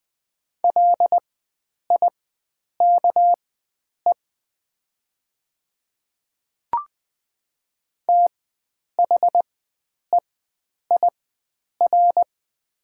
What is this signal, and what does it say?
Morse code, heard as a keyed tone, about 20 wpm, spaced as for 9 wpm, 700 Hz. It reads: LIKE THEIR
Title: 4x Character Spacing - Top 300 Words - 20wpm